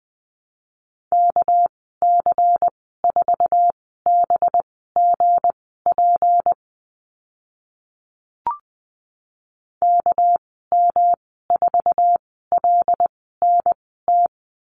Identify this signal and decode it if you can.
Morse code, heard as a keyed tone, about 20 wpm, spaced as for 14 wpm, 700 Hz. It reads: KC4BGP KM4LNT
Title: 231 Lesson 37g - Letter x - Call Signs 20wpm